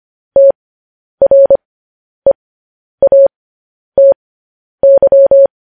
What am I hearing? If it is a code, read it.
Morse code, heard as a keyed tone, about 25 wpm, spaced as for 10 wpm, 550 Hz. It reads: TREATY
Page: news headlines in morse code at 10 wpm